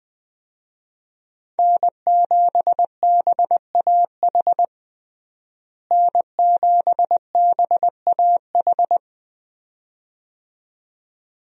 Morse code, heard as a keyed tone, about 21 wpm, 700 Hz. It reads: N7BAH N7BAH
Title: USA Call Signs - 5 Characters - Repeated 2x - 20wpm